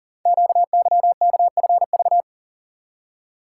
Morse code, heard as a keyed tone, about 38 wpm, 700 Hz. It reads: QYKFV
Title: Code Groups - 5 Letters - 40wpm